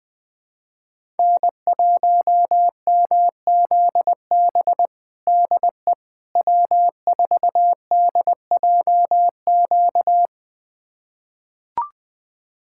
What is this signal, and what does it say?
Morse code, heard as a keyed tone, about 20 wpm, 700 Hz. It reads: N1MZB DE W4DJQ
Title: USA Callsign DE Callsign - 5 Characters - 20wpm